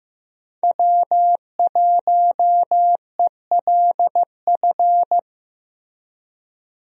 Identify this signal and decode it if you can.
Morse code, heard as a keyed tone, about 16 wpm, 700 Hz. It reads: W1ELF
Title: USA Call Signs - 5 Characters - Repeated 2x - 15wpm